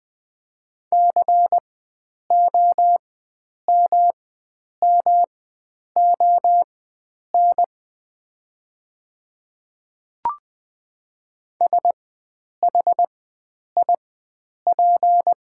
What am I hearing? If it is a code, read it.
Morse code, heard as a keyed tone, about 20 wpm, spaced as for 9 wpm, 700 Hz. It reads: COMMON SHIP
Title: 4x Character Spacing - Top 500 Words - 20wpm